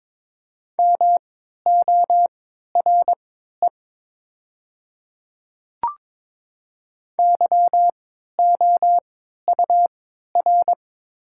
Morse code, heard as a keyed tone, about 22 wpm, spaced as for 12 wpm, 700 Hz. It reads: MORE YOUR
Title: 3x Character Spacing - Top 100 Words - 22wpm